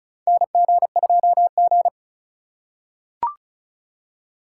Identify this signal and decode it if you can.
Morse code, heard as a keyed tone, about 34 wpm, 700 Hz. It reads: NG2G
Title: USA Call Signs - 4 Characters - 2x Character Spacing - 35wpm